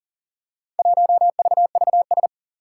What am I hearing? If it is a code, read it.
Morse code, heard as a keyed tone, about 40 wpm, 700 Hz. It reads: 1VVS
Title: Four Challenging Letter-Number Combinations 40wpm